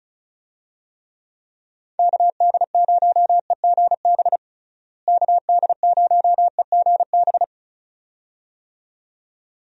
Morse code, heard as a keyed tone, about 36 wpm, 700 Hz. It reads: KD0EGB KD0EGB
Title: USA Call Signs - 6 Characters - Repeated 2x - 35wpm